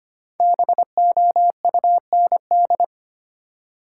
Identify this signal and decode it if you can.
Morse code, heard as a keyed tone, about 25 wpm, 700 Hz. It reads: BOUND
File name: 5 Letter Words - First 3 Letters Given - 25wpm